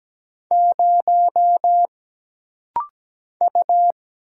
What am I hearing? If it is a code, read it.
Morse code, heard as a keyed tone, about 17 wpm, 700 Hz. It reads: 0 U
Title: Single Letter-Number - Rapid-Fire 17wpm